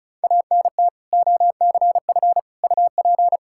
Morse code, heard as a keyed tone, about 36 wpm, 700 Hz. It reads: ANT OCF UP
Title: Practice QSO - Exchange 2 - Info - 35wpm